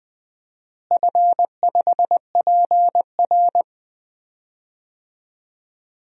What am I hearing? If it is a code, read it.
Morse code, heard as a keyed tone, about 20 wpm, 700 Hz. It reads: F5PR
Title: DXCC-100 Call Signs - 4 Characters - Repeated 3x - 20wpm